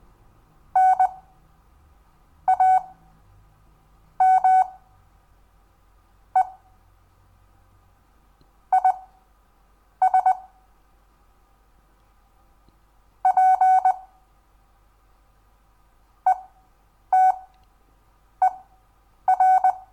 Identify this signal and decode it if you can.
Morse code, heard as a keyed tone, about 20 wpm, spaced as for 6 wpm, 750 Hz. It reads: NAME IS PETER
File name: cw / Lesson10 REVIEW